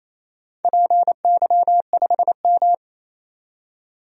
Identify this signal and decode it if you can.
Morse code, heard as a keyed tone, about 28 wpm, 700 Hz. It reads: PY5M
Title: DXCC-100 Call Signs - 4 Characters - 28wpm